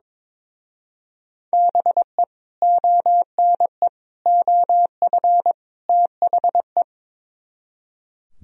Morse code, heard as a keyed tone, about 22 wpm, 700 Hz. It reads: BE ONE OF THE
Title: Speed-Racing - Sets of 4 Words 22wpm